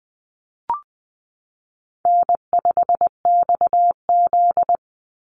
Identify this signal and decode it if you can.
Morse code, heard as a keyed tone, about 20 wpm, 700 Hz. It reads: N5XZ
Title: Easy Call Signs Repeated 2x - 20wpm